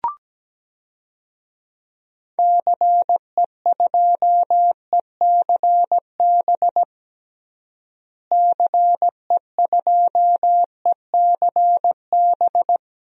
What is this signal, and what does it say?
Morse code, heard as a keyed tone, about 17 wpm, 700 Hz. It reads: CE2ECB CE2ECB
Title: DXCC-100 Call Signs - 6 Characters - Repeated 3x - 17wpm